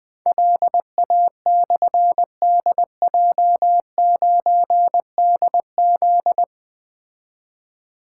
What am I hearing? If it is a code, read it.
Morse code, heard as a keyed tone, about 20 wpm, 700 Hz. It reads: LA/DJ9DZ
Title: Difficult Call Signs 20wpm